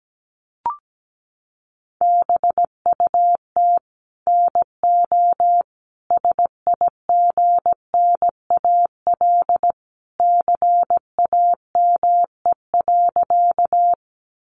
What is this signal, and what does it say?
Morse code, heard as a keyed tone, about 17 wpm, 700 Hz. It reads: BUT NO SIGNAL CAME.
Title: Sentences from Top 1400 Words 17wpm